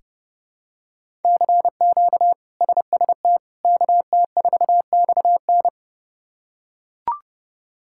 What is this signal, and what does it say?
Morse code, heard as a keyed tone, about 30 wpm, 700 Hz. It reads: CQ SST KT4XN
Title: K1USN SST QSO Practice - 30wpm